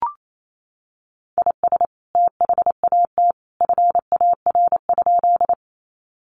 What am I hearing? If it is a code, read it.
Morse code, heard as a keyed tone, about 28 wpm, 700 Hz. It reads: IS THAT FAR?
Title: Sentences from Top 300 Words 28wpm